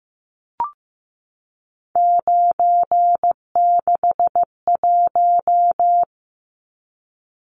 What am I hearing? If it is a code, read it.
Morse code, heard as a keyed tone, about 15 wpm, 700 Hz. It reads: 961
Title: Numbers - 3-Digit 15wpm